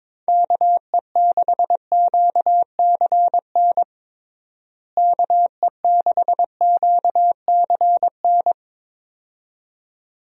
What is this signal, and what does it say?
Morse code, heard as a keyed tone, about 21 wpm, 700 Hz. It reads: KE6QCN KE6QCN
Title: USA Call Signs - 6 Characters - Repeated 2x - 22wpm